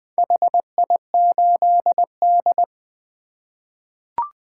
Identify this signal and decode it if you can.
Morse code, heard as a keyed tone, about 20 wpm, 700 Hz. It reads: HI8D
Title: 2023 POTA Activator Call Signs - 20wpm